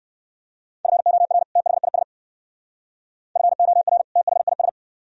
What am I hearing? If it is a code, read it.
Morse code, heard as a keyed tone, about 54 wpm, 700 Hz. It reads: FOR THIS FOR THIS
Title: Speed-Racing - Sets of 2 Words 45wpm